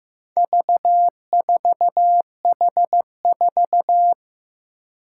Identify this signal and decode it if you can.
Morse code, heard as a keyed tone, about 15 wpm, 700 Hz. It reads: V4H4
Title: Four Challenging Letter-Number Combinations 15wpm